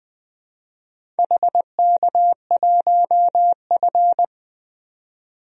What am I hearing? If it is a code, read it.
Morse code, heard as a keyed tone, about 20 wpm, 700 Hz. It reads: HK1F
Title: DXCC-100 Call Signs - 4 Characters - 20wpm